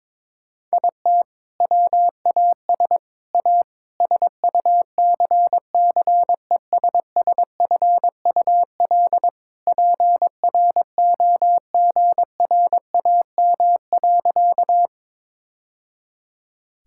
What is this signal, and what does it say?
Morse code, heard as a keyed tone, about 22 wpm, 700 Hz. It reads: IT WAS A SUCCESSFUL PROGRAM.
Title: Sentences from Top 1000 Words - Review Part 5 - 22wpm